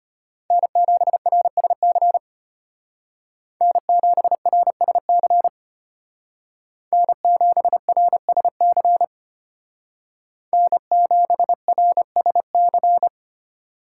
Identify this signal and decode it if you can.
Morse code, heard as a keyed tone, about 30 wpm, 700 Hz. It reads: N7RSC N7RSC N7RSC N7RSC
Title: Speed Racing - USA Call Signs 25wpm